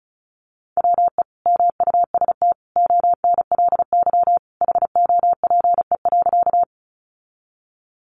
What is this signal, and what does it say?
Morse code, heard as a keyed tone, about 34 wpm, 700 Hz. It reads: WE MUST ONLY HOPE.
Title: Sentences from Top 1000 Words 35wpm